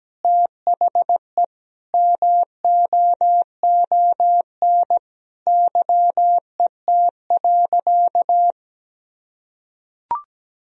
Morse code, heard as a keyed tone, about 17 wpm, 700 Hz. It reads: THE MOON YET.
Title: Sentences from Top 1000 Words - Review Part 3 - 17wpm